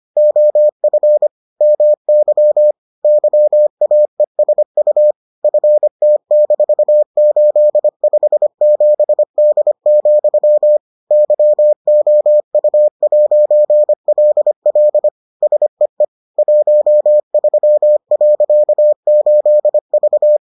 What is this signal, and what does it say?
Morse code, heard as a keyed tone, about 25 wpm, 600 Hz. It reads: OF MY YAESU FT-857D, YOU'LL SEE 13.8V